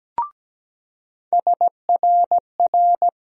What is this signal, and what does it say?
Morse code, heard as a keyed tone, about 17 wpm, 700 Hz. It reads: SRR
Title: USA Call Sign Postfixes - 3 Letters 17wpm